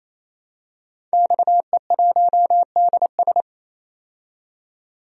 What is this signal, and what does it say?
Morse code, heard as a keyed tone, about 28 wpm, 700 Hz. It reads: XE1DS
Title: DXCC-100 Call Signs - 5 Characters - 28wpm